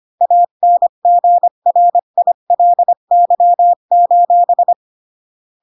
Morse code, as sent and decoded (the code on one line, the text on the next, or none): .- -. --. .-. .. .-.. -.-- ---...
ANGRILY: